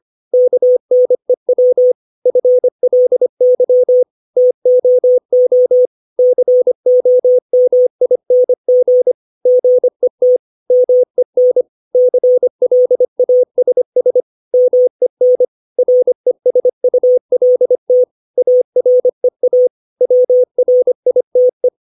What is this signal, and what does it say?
Morse code, heard as a keyed tone, about 25 wpm, 500 Hz. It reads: KNEW FLY TOO COMING GET MEN CLASS MEN RESULT AREA WRITE
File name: 632 words / 632 Words 25WPM